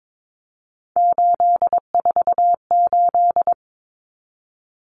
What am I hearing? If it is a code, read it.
Morse code, heard as a keyed tone, about 22 wpm, 700 Hz. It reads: 848
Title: Common Cut Numbers - 3-Digit 22wpm